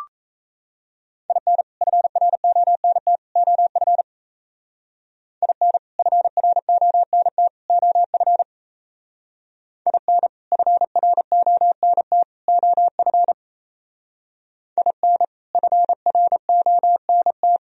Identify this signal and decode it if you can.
Morse code, heard as a keyed tone, about 36 wpm, 700 Hz. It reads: IN FRONT OF IN FRONT OF IN FRONT OF IN FRONT